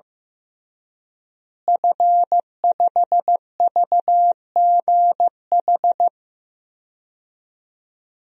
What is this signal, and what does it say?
Morse code, heard as a keyed tone, about 15 wpm, 700 Hz. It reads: F5VGH
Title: DXCC-100 Call Signs - 5 Characters - Repeated 3x - 15wpm